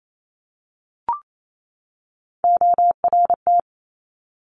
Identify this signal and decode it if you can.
Morse code, heard as a keyed tone, about 28 wpm, 700 Hz. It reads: ORT
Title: Top Three Letter Combinations in Words 28wpm